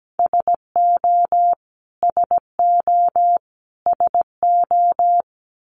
Morse code, heard as a keyed tone, about 17 wpm, 700 Hz. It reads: SO SO SO
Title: Top 200 Words Repeated 17wpm